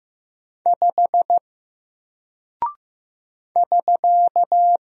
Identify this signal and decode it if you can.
Morse code, heard as a keyed tone, about 15 wpm, 700 Hz. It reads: 5 <SK>
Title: Single Character (Letter:Number:Punctuation:Prosign) Alt 15wpm